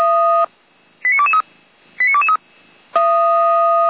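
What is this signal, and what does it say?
Five-tone selective call (ZVEI), heard as quick runs of five tones, tones of about 70 ms. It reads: 89223 89223
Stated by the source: ZVEI selective call received by SDR, fire brigade audio archive of feuerwehr-orlamuende.org.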